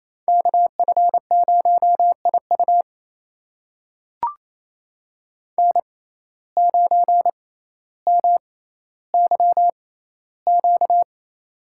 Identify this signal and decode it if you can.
Morse code, heard as a keyed tone, about 28 wpm, spaced as for 14 wpm, 700 Hz. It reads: KF0IU N9MYQ
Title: USA Call Signs - 5 Characters - 6x Character Spacing - 28wpm